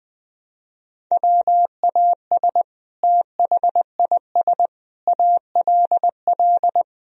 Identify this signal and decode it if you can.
Morse code, heard as a keyed tone, about 20 wpm, 700 Hz. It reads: WAS THIS ALL
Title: Sentences from Top 1000 Words - Review Part 3 - 20wpm